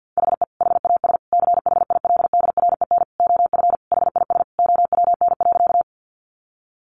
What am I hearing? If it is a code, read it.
Morse code, heard as a keyed tone, about 50 wpm, 700 Hz. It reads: HE HAS CHILDREN OF HIS OWN.
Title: Sentences from Top 300 Words 50wpm